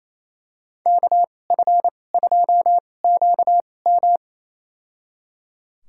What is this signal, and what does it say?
Morse code, heard as a keyed tone, about 28 wpm, spaced as for 20 wpm, 700 Hz. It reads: KF2QM
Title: USA Call Signs - 5 Characters - 2x Character Spacing - 28wpm